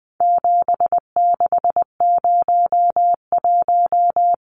Morse code, heard as a keyed tone, about 20 wpm, 700 Hz. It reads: 7601
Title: Numbers - 4-Digit 20wpm